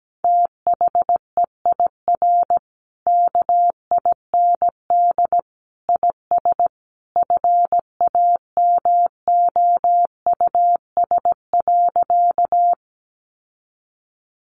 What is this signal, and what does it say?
Morse code, heard as a keyed tone, about 17 wpm, 700 Hz. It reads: THEIR KIND IS FAMOUS.
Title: Sentences from Top 1500 Words 17wpm